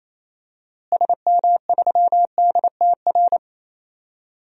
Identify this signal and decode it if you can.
Morse code, heard as a keyed tone, about 28 wpm, 700 Hz. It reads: SM3DTR
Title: DXCC-100 Call Signs - 6 Characters - Repeated 3x - 28wpm